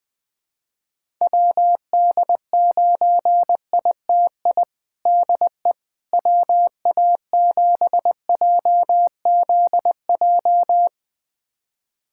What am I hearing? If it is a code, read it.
Morse code, heard as a keyed tone, about 20 wpm, 700 Hz. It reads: WD9ITI DE WA7JZJ
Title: USA Callsign DE Callsign - 6 Characters - 20wpm